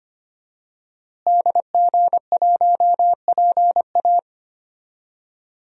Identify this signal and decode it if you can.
Morse code, heard as a keyed tone, about 25 wpm, 700 Hz. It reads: DG1PA
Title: DXCC-100 Call Signs - 5 Characters - Repeated 2x - 25wpm